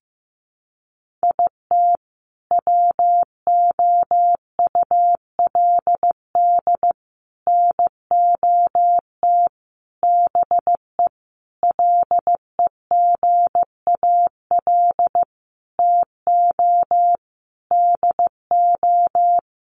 Morse code, heard as a keyed tone, about 15 wpm, 700 Hz. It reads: IT WOULD NOT BE LEGAL TO DO
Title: Sentences from Top 1200 Words 15wpm